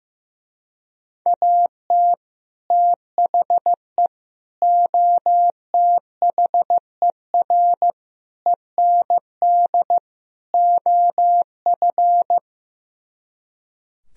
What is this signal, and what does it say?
Morse code, heard as a keyed tone, about 15 wpm, 700 Hz. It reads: AT THE OTHER END OF
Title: Speed-Racing - Sets of 5 Words 15wpm